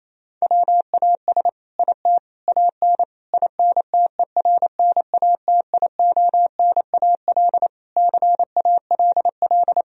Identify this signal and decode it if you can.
Morse code, heard as a keyed tone, about 28 wpm, 700 Hz. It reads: WAS IT AN INTERNATIONAL CALL
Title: Sentences from Top 1000 Words - Review Part 4 - 28wpm